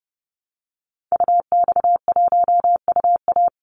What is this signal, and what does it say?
Morse code, heard as a keyed tone, about 30 wpm, 700 Hz. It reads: UX1UA
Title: Call Signs - 30wpm